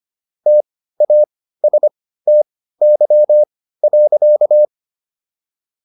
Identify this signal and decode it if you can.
Morse code, heard as a keyed tone, about 25 wpm, spaced as for 15 wpm, 600 Hz. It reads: TASTY.